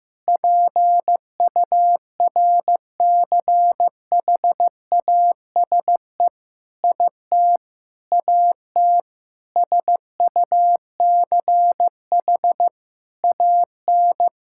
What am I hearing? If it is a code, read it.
Morse code, heard as a keyed tone, about 15 wpm, 700 Hz. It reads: PURCHASE IT AT SUCH AN
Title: Sentences from Top 1000 Words - Review Part 5 - 15wpm